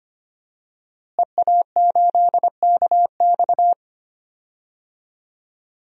Morse code, heard as a keyed tone, about 25 wpm, 700 Hz. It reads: EA8KX
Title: DXCC-100 Call Signs - 5 Characters - Repeated 3x - 25wpm